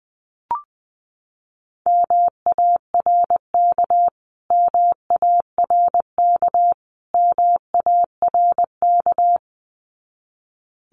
Morse code, heard as a keyed tone, about 20 wpm, 700 Hz. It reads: MARK MARK MARK
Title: Top 500 Words Repeated 20wpm